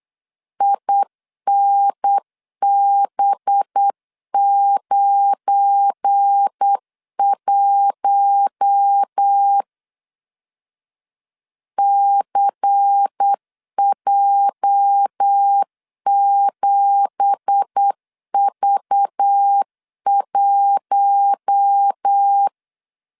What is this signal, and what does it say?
Morse code, heard as a keyed tone, about 8 wpm, 800 Hz. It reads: INB91 CJ7V1
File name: 8wpm